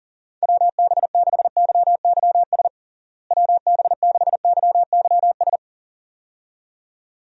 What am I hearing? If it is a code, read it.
Morse code, heard as a keyed tone, about 37 wpm, 700 Hz. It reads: WB6YYS WB6YYS
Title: USA Call Signs - 6 Characters - Repeated 2x - 40wpm